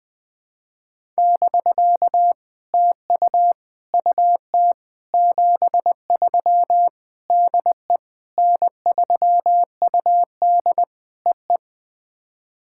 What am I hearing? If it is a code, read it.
Morse code, heard as a keyed tone, about 20 wpm, 700 Hz. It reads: <BK> TU UT 73 DE N3UD EE
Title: POTA QSO Practice - 20wpm